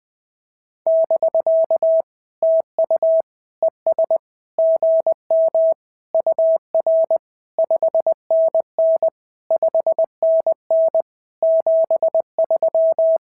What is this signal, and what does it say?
Morse code, heard as a keyed tone, about 19 wpm, 650 Hz. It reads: <BK> TU ES GM UR 5NN 5NN 73
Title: SOTA QSO Practice 20wpm